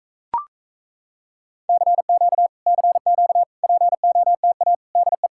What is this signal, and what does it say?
Morse code, heard as a keyed tone, about 38 wpm, 700 Hz. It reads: CQ CQ POTA DE